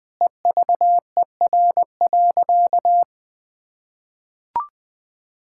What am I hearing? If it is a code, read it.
Morse code, heard as a keyed tone, about 20 wpm, 700 Hz. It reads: EVER.
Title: Sentences from Top 1600 Words 20wpm